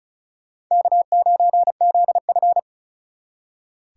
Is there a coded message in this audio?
Yes: Morse code from 0.7 to 2.6 s, heard as a keyed tone, about 46 wpm, 700 Hz. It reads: K9ZF